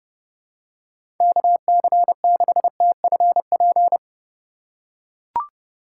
Morse code, heard as a keyed tone, about 30 wpm, 700 Hz. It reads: KC6TFP
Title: USA Call Signs - 6 Characters - 4x Character Spacing - 30wpm